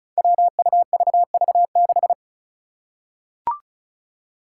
Morse code, heard as a keyed tone, about 36 wpm, 700 Hz. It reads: WUVV6
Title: Five Challenging Letter-Number Combinations 35wpm